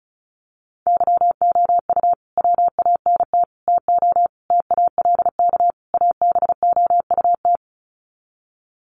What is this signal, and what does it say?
Morse code, heard as a keyed tone, about 34 wpm, 700 Hz. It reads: YOU WANT TO TALK ABOUT
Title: Sets of 5 Words Encore 35wpm